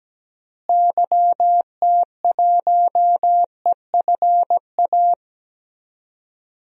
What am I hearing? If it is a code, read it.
Morse code, heard as a keyed tone, about 17 wpm, 700 Hz. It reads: YT1EFA
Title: DXCC-100 Call Signs - 6 Characters - Repeated 2x - 17wpm